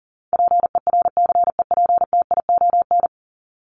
Morse code, heard as a keyed tone, about 40 wpm, 700 Hz. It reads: PERCEPTION
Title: Top 1751-2000 Words 40wpm